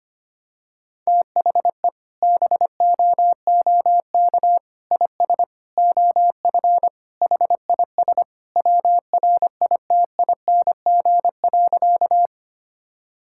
Morse code, heard as a keyed tone, about 25 wpm, 700 Hz. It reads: THE BOOK IS OF HIS WRITING.
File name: practice / 25wpm / Sentences with 2-Word Phrases - 25wpm